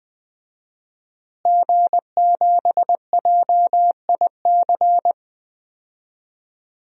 Morse code, heard as a keyed tone, about 20 wpm, 700 Hz. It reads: G7JIC
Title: DXCC-100 Call Signs - 5 Characters - Repeated 2x - 20wpm